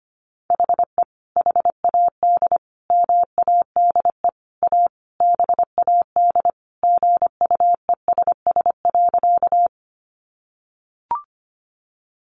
Morse code, heard as a keyed tone, about 25 wpm, 700 Hz. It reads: HE HAD MADE A BAD GUESS.
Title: Sentences from Top 1500 Words 25wpm